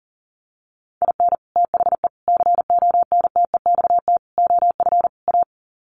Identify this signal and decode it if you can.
Morse code, heard as a keyed tone, about 40 wpm, 700 Hz. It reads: IN THE CONTEXT OF A
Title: Sets of 5 Words Encore 40wpm